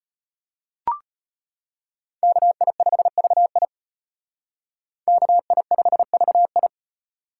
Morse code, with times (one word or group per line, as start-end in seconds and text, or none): 2.2-3.7 KI5VI
5.1-6.7 KI5VI